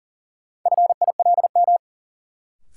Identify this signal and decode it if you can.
Morse code, heard as a keyed tone, about 40 wpm, 700 Hz. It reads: FILM
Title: Sentences from Top 1000 Words - Review Part 3 - New Word Drill - 40wpm